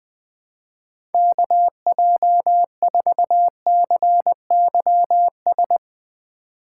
Morse code, heard as a keyed tone, about 21 wpm, 700 Hz. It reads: KJ4CYS
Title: USA Call Signs - 2x Character Spacing - 20wpm